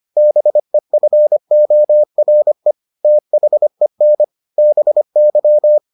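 Morse code, heard as a keyed tone, about 25 wpm, 600 Hz. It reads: BEFORE THEN BY
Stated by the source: Morse edition of the Foundations of Amateur Radio episodes.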